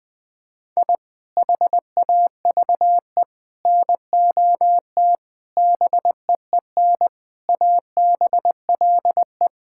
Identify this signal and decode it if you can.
Morse code, heard as a keyed tone, about 20 wpm, 700 Hz. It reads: I HAVE NOT BEEN ABLE
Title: Sentences from Top 1000 Words - Review Part 6 - 20wpm